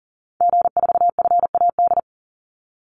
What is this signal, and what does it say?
Morse code, heard as a keyed tone, about 40 wpm, 700 Hz. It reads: G4FAD
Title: Call Signs - 40wpm